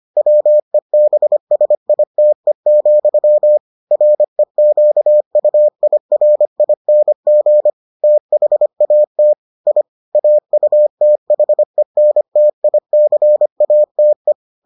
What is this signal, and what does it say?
Morse code, heard as a keyed tone, about 25 wpm, 600 Hz. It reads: WEBSITE, REQUIRING THAT I AUTHENTICATE